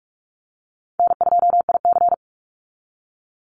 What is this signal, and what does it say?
Morse code, heard as a keyed tone, about 44 wpm, 700 Hz. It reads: N2IC
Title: Easy Call Signs Repeated 3x - 45wpm